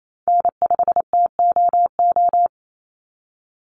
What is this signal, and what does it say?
Morse code, heard as a keyed tone, about 28 wpm, 700 Hz. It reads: N5TOO
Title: Call Signs - 28wpm